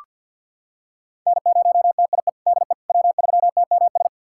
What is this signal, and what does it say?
Morse code, heard as a keyed tone, about 47 wpm, 700 Hz. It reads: N0TIE DE W3TMS
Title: USA Callsign DE Callsign - 5 Characters - 50wpm